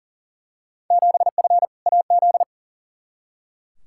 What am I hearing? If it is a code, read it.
Morse code, heard as a keyed tone, about 40 wpm, 700 Hz. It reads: 7F AZ
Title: Field Day Exchanges Encore - 40wpm